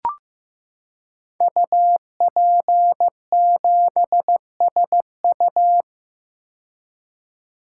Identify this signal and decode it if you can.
Morse code, heard as a keyed tone, about 15 wpm, 700 Hz. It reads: UP7SU
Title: Code Groups - 5 Letter:Numbers - 15wpm